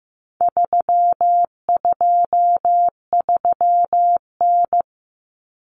Morse code, heard as a keyed tone, about 15 wpm, 700 Hz. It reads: 323N
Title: Common Cut Numbers - 4-Digit 15wpm